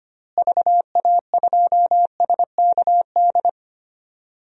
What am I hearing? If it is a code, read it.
Morse code, heard as a keyed tone, about 25 wpm, 700 Hz. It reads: VA2SKD
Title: DXCC-100 Call Signs - 6 Characters - Repeated 2x - 25wpm